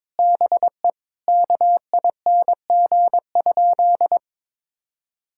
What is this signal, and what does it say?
Morse code, heard as a keyed tone, about 23 wpm, 700 Hz. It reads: BE KING?
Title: Sentences from Top 1600 Words 22wpm